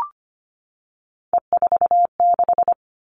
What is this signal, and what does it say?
Morse code, heard as a keyed tone, about 25 wpm, 700 Hz. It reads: E46
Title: All Cut Numbers - 3-Digit 25wpm